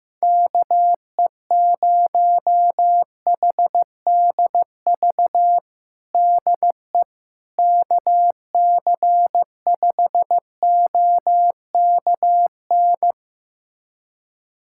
Morse code, heard as a keyed tone, about 15 wpm, 700 Hz. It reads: KE0HDV DE KC5OKN
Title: USA Callsign DE Callsign 15wpm